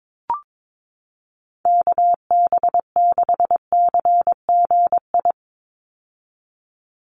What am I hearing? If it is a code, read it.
Morse code, heard as a keyed tone, about 22 wpm, 700 Hz. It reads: KB6CGI
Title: USA Call Signs 22wpm